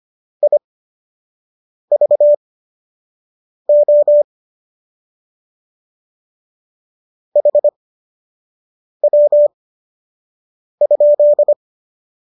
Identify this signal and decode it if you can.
Morse code, heard as a keyed tone, about 25 wpm, spaced as for 6 wpm, 590 Hz. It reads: IVO HW?
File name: qso012 06